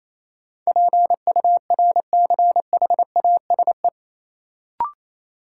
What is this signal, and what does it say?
Morse code, heard as a keyed tone, about 28 wpm, 700 Hz. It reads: PURCHASE
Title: Sentences from Top 800 Words - New Word Drill - 28wpm